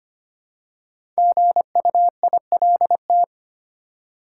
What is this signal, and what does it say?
Morse code, heard as a keyed tone, about 25 wpm, 700 Hz. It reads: GUILT
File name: 5 Letter Words - First 3 Letters Given - 25wpm